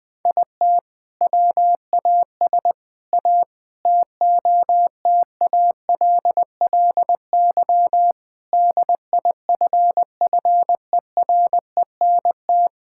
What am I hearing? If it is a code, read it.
Morse code, heard as a keyed tone, about 20 wpm, 700 Hz. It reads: IT WAS A TOTALLY DIFFERENT
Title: Sentences from Top 1600 Words 20wpm